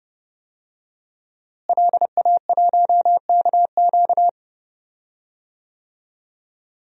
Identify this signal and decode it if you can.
Morse code, heard as a keyed tone, about 31 wpm, 700 Hz. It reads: LA1KQ
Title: DXCC-100 Call Signs - 5 Characters - Repeated 3x - 30wpm